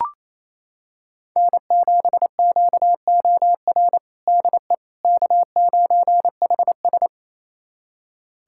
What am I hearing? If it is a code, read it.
Morse code, heard as a keyed tone, about 28 wpm, 700 Hz. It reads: N7QOR DE K9HS
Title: USA Callsign DE Callsign 28wpm